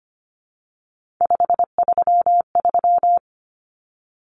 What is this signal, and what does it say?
Morse code, heard as a keyed tone, about 25 wpm, 700 Hz. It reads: 533
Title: Numbers - 3-Digit 25wpm